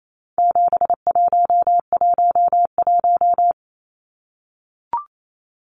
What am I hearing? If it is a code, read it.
Morse code, heard as a keyed tone, about 28 wpm, 700 Hz. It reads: 7111
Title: Numbers - 4-Digit 28wpm